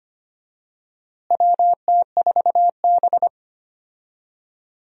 Morse code, heard as a keyed tone, about 25 wpm, 700 Hz. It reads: WT4B
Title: USA Call Signs - 4 Characters - Repeated 3x - 25wpm